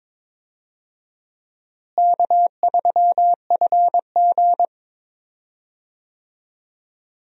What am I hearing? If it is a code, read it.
Morse code, heard as a keyed tone, about 22 wpm, 700 Hz. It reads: K3FG
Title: USA Call Signs - 4 Characters - Repeated 2x - 22wpm